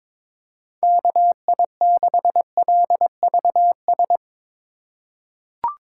Morse code, heard as a keyed tone, about 22 wpm, 700 Hz. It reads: KI6LVS
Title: USA Call Signs - 6 Characters - 2x Character Spacing - 22wpm